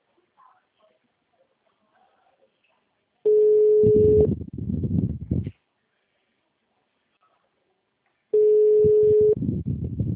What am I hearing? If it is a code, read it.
Ringback tone (Europe, 425 Hz).